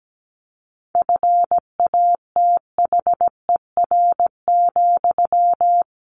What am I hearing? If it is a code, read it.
Morse code, heard as a keyed tone, about 17 wpm, 700 Hz. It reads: FATHER,
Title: Sentences from Top 1200 Words 17wpm